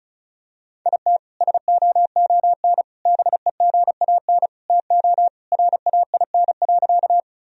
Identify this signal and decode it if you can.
Morse code, heard as a keyed tone, about 36 wpm, 700 Hz. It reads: IT SOON BEGAN TO RAIN.
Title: Sentences from Top 1000 Words - Review Part 1 - 35wpm